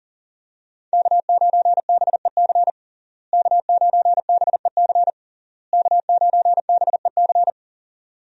Morse code, heard as a keyed tone, about 40 wpm, 700 Hz. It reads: K9BEC K9BEC K9BEC
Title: USA Call Signs - 5 Characters - Repeated 3x - 40wpm